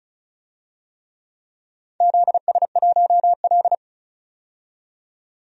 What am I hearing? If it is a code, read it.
Morse code, heard as a keyed tone, about 34 wpm, 700 Hz. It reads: ZS1L
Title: DXCC-100 Call Signs - 4 Characters - 35wpm